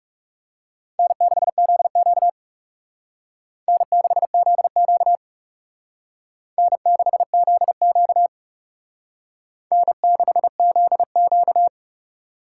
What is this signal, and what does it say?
Morse code, heard as a keyed tone, about 36 wpm, 700 Hz. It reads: N6ZQ N6ZQ N6ZQ N6ZQ
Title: Speed Racing - USA 1x2 Call Signs 30wpm